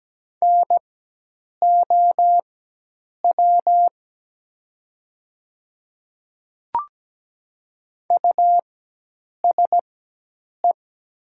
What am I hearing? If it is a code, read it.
Morse code, heard as a keyed tone, about 17 wpm, spaced as for 8 wpm, 700 Hz. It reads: NOW USE
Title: 4x Character Spacing - Top 100 Words - 17wpm